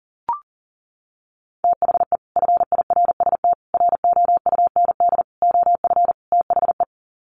Morse code, heard as a keyed tone, about 40 wpm, 700 Hz. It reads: THE FIRST ROUND OF THE
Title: Sets of 5 Words Encore 40wpm